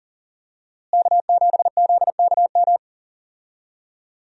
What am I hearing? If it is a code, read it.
Morse code, heard as a keyed tone, about 40 wpm, 700 Hz. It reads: K7ZKM
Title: USA Call Signs - 5 Characters - Repeated 2x - 40wpm